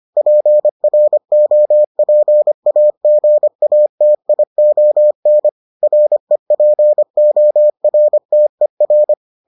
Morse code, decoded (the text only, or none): PROPAGATION REPORTER